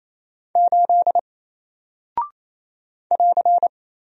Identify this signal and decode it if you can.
Morse code, heard as a keyed tone, about 28 wpm, 700 Hz. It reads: 8 <AR>
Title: Single Character (Letter:Number:Punctuation:Prosign) 28wpm